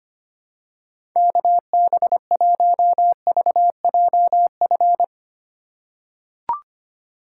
Morse code, heard as a keyed tone, about 25 wpm, 700 Hz. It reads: KB1VJF
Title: USA Call Signs - 6 Characters - 2x Character Spacing - 25wpm